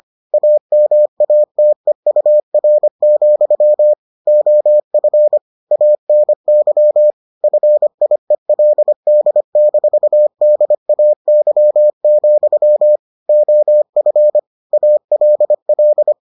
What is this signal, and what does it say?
Morse code, heard as a keyed tone, about 25 wpm, 600 Hz. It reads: AMATEUR, OF ANY FIELD-DAY, OF ALL